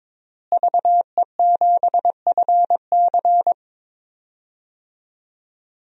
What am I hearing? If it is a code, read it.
Morse code, heard as a keyed tone, about 22 wpm, 700 Hz. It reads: VE7FC
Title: DXCC-100 Call Signs - 5 Characters - Repeated 2x - 22wpm